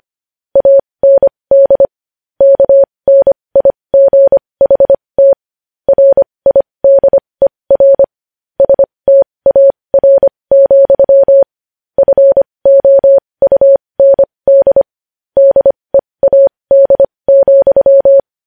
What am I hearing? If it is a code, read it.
Morse code, heard as a keyed tone, about 25 wpm, spaced as for 20 wpm, 550 Hz. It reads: AND KNIGHT RIDER STAR, FOUND DEAD,
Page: news-headlines-in-morse-code-at-20-wpm